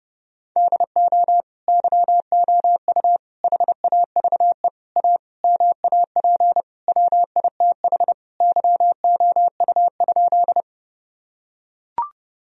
Morse code, heard as a keyed tone, about 30 wpm, 700 Hz. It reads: DO YOU HAVE A MAP WITH YOU?
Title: Sentences from Top 1000 Words - Review Part 2 - 30wpm